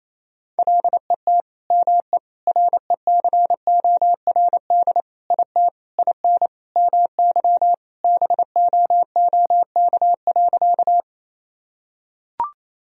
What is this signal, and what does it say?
Morse code, heard as a keyed tone, about 28 wpm, 700 Hz. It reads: LET ME RECORD IT IN MY BOOK.
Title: Sentences from Top 1000 Words - Review Part 3 - 28wpm